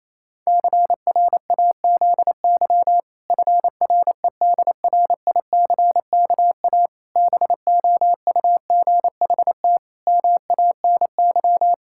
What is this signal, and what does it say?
Morse code, heard as a keyed tone, about 28 wpm, 700 Hz. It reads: CRAZY FREDRICKA BOUGHT MANY
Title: Pangram Challenge v2 - 28wpm